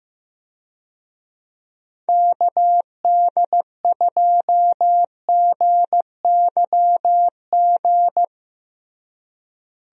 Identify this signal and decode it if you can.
Morse code, heard as a keyed tone, about 16 wpm, 700 Hz. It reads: KD2GYG
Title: USA Call Signs - 6 Characters - Repeated 2x - 15wpm